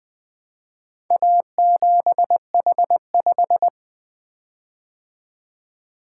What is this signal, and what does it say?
Morse code, heard as a keyed tone, about 20 wpm, 700 Hz. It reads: A7H5
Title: Four Letter-Number Combinations - 20wpm